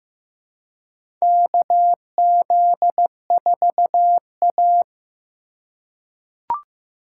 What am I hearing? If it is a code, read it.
Morse code, heard as a keyed tone, about 15 wpm, 700 Hz. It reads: KZ4A
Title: K1USN SST QSO Practice - 15wpm